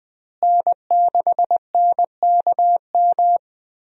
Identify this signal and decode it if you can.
Morse code, heard as a keyed tone, about 20 wpm, 700 Hz. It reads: N6NKM
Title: USA Call Signs - 5 Characters - Repeated 2x - 20wpm